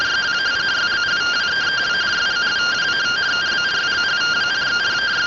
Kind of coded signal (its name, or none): two-tone teleprinter signal